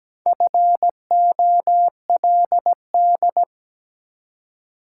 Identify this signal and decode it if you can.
Morse code, heard as a keyed tone, about 17 wpm, 700 Hz. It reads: FOLD
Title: Four Letter Words - 17wpm